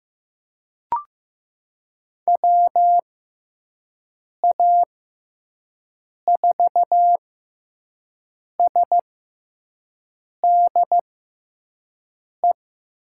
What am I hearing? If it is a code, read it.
Morse code, heard as a keyed tone, about 15 wpm, spaced as for 5 wpm, 700 Hz. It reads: WA4SDE
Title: USA Call Signs - 6 Characters - 6x Character Spacing - 15wpm